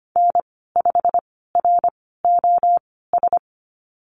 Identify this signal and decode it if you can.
Morse code, heard as a keyed tone, about 25 wpm, spaced as for 16 wpm, 700 Hz. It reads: N5ROS
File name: Lesson 15g - Number 5 - Call Signs 25wpm